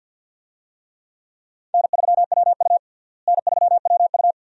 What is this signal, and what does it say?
Morse code, heard as a keyed tone, about 47 wpm, 700 Hz. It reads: N3WU N3WU
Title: USA Call Signs - 4 Characters - Repeated 2x - 50wpm